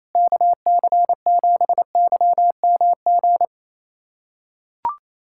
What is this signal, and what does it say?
Morse code, heard as a keyed tone, about 28 wpm, 700 Hz. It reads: KC7YMG
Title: USA Callsign DE Callsign - 6 Characters - 28wpm